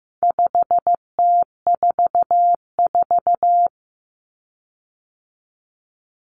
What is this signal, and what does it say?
Morse code, heard as a keyed tone, about 15 wpm, 700 Hz. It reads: 5T44
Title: Common Cut Numbers - 4-Digit 15wpm